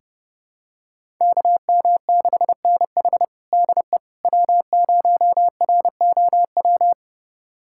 Morse code, heard as a keyed tone, about 32 wpm, 700 Hz. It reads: KM6NH DE W0ROW
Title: USA Callsign DE Callsign - 5 Characters - 30wpm